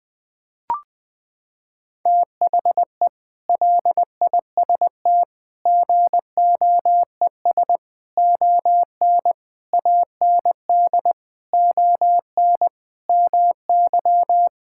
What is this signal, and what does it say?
Morse code, heard as a keyed tone, about 21 wpm, 700 Hz. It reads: THE LIST GOES ON AND ON MY
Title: Sentences with 6-Word Phrases - 20wpm